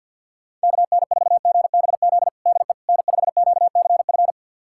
Morse code, heard as a keyed tone, about 51 wpm, 700 Hz. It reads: KN4GBZ DE N5QCF
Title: USA Callsign DE Callsign 50wpm